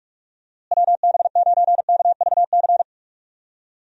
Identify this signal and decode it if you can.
Morse code, heard as a keyed tone, about 47 wpm, 700 Hz. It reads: WB9KVC
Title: USA Call Signs - 6 Characters - Repeated 3x - 45wpm